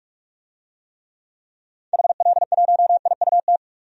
Morse code, heard as a keyed tone, about 42 wpm, 700 Hz. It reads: HL1IUT
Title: DXCC-100 Call Signs - 6 Characters - 45wpm